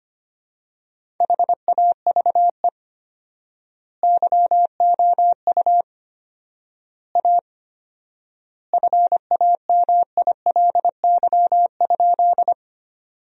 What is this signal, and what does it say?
Morse code, heard as a keyed tone, about 25 wpm, 700 Hz. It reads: HAVE YOU A FAMILY?
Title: 4x Word-Spacing - Sentences from Top 400 Words - 25wpm